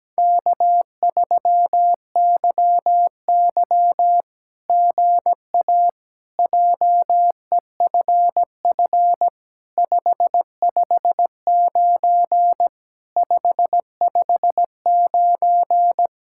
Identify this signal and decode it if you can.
Morse code, heard as a keyed tone, about 17 wpm, 700 Hz. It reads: K3YY GA JEFF 559 559
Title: SOTA QSO Practice 17wpm